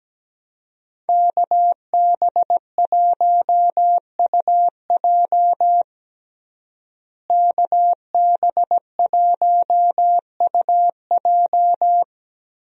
Morse code, heard as a keyed tone, about 17 wpm, 700 Hz. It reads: KB1UJ KB1UJ
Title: USA Call Signs - 5 Characters - Repeated 3x - 17wpm